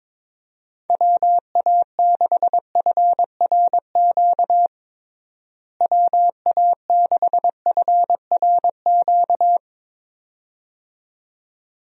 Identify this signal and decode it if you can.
Morse code, heard as a keyed tone, about 22 wpm, 700 Hz. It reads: WA6FRQ WA6FRQ
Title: USA Call Signs - 6 Characters - Repeated 2x - 22wpm